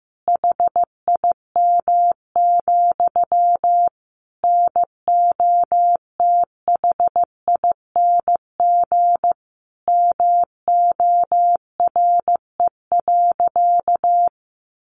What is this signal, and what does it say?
Morse code, heard as a keyed tone, about 15 wpm, 700 Hz. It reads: HIM, NOTHING MORE.
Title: Sentences from Top 1300 Words 15wpm